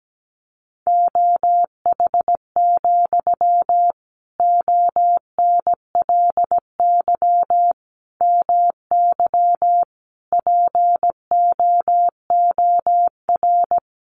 Morse code, heard as a keyed tone, about 17 wpm, 700 Hz. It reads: OH, ONLY MY POOR
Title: Sentences from Top 1000 Words 17wpm